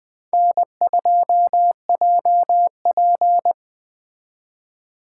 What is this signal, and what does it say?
Morse code, heard as a keyed tone, about 20 wpm, 700 Hz. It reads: N2JP
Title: Speed Racing - USA 1x2 Call Signs 17wpm